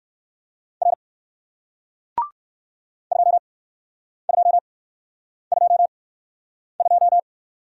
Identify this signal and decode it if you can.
Morse code, heard as a keyed tone, about 61 wpm, spaced as for 9 wpm, 700 Hz. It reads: R 2222